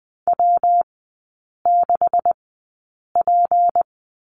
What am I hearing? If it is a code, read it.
Morse code, heard as a keyed tone, about 20 wpm, spaced as for 8 wpm, 700 Hz. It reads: W6P
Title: 184 Lesson 30e - Letter b - Three Characters 20wpm